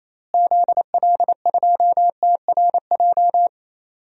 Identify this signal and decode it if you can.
Morse code, heard as a keyed tone, about 28 wpm, 700 Hz. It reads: ZL2TRJ
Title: DXCC-100 Call Signs - 6 Characters - Repeated 2x - 28wpm